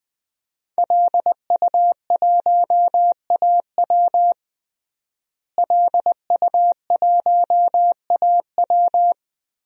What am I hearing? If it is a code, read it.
Morse code, heard as a keyed tone, about 20 wpm, 700 Hz. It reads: LU1AW LU1AW
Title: DXCC-100 Call Signs - 5 Characters - Repeated 2x - 20wpm